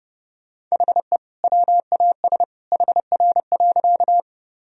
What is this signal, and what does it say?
Morse code, heard as a keyed tone, about 30 wpm, 700 Hz. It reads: HE WAS HR.
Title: Sentences from Top 200 Words and Abbreviations 30wpm